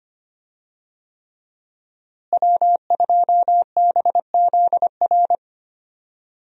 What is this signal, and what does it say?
Morse code, heard as a keyed tone, about 25 wpm, 700 Hz. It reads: W2BZR
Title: USA Call Signs - 5 Characters - Repeated 3x - 25wpm